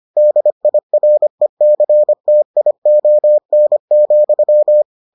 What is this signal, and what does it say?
Morse code, heard as a keyed tone, about 25 wpm, 600 Hz. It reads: DIRECTION,